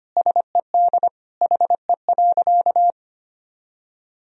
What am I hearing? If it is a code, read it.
Morse code, heard as a keyed tone, about 25 wpm, 700 Hz. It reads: SED HE.
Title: Sentences from Top 100 Words and Abbreviations 25wpm